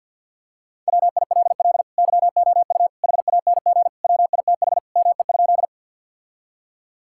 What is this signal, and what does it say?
Morse code, heard as a keyed tone, about 50 wpm, 700 Hz. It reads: WILL YOU HANG WITH ME?
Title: Sentences from Top 1600 Words 50wpm